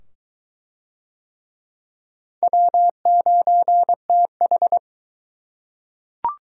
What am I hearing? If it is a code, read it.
Morse code, heard as a keyed tone, about 22 wpm, 700 Hz. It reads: W9TH